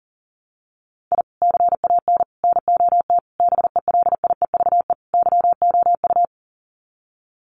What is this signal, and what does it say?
Morse code, heard as a keyed tone, about 40 wpm, 700 Hz. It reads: I CAN NOT BELIEVE YOU
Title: Sets of 5 Words Encore 40wpm